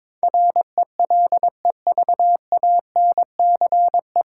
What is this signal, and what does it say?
Morse code, heard as a keyed tone, about 22 wpm, 700 Hz. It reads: RELEVANCE
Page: Morse Code Ninja practice 22wpm